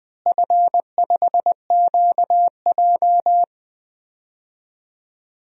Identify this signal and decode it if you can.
Morse code, heard as a keyed tone, about 20 wpm, 700 Hz. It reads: F5QJ
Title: DXCC-100 Call Signs - 4 Characters - Repeated 3x - 20wpm